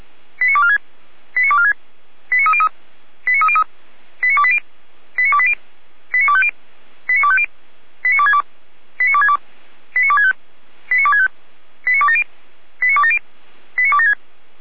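Five-tone selective call (ZVEI), heard as quick runs of five tones, tones of about 70 ms. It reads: 89246 89246 89303 89303 89280 89280 89250 89250 89262 89262 89265 89265 89270 89270 89276
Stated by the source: Five-tone ZVEI selective call received on an SDR.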